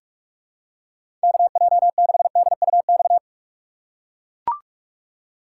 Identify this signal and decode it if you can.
Morse code, heard as a keyed tone, about 43 wpm, 700 Hz. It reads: KJ6DUX
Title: USA Call Signs - 6 Characters - 3x Character Spacing - 45wpm